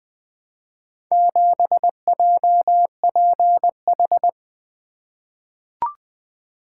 Morse code, heard as a keyed tone, about 20 wpm, 700 Hz. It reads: 7JPH
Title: Four Letter-Number Combinations - 20wpm